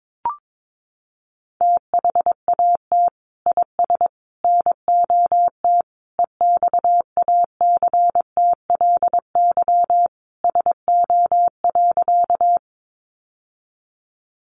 Morse code, heard as a keyed tone, about 22 wpm, 700 Hz. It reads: THAT IS NOT EXACTLY SO.
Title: Sentences from Top 900 Words 22wpm